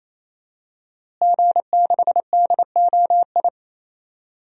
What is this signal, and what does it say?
Morse code, heard as a keyed tone, about 28 wpm, 700 Hz. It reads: G6DOI
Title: DXCC-100 Call Signs - 5 Characters - Repeated 3x - 28wpm